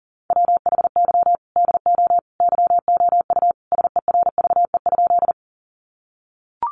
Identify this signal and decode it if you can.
Morse code, heard as a keyed tone, about 40 wpm, 700 Hz. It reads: WHY DO YOU SERVE?
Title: Sentences from Top 400 Words 40wpm